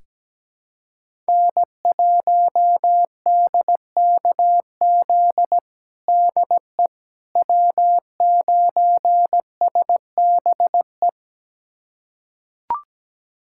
Morse code, heard as a keyed tone, about 17 wpm, 700 Hz. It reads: N1DKZ DE W9SBE
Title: USA Callsign DE Callsign - 5 Characters - 17wpm